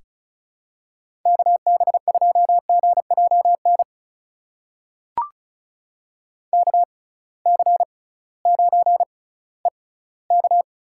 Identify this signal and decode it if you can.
Morse code, heard as a keyed tone, about 36 wpm, spaced as for 19 wpm, 700 Hz. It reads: KB2GJN KC9EK
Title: USA Call Signs - 6 Characters - 6x Character Spacing - 35wpm